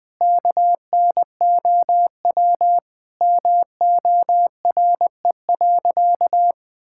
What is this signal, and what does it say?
Morse code, heard as a keyed tone, about 20 wpm, 700 Hz. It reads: KNOW MORE.
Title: Sentences with 6-Word Phrases - 20wpm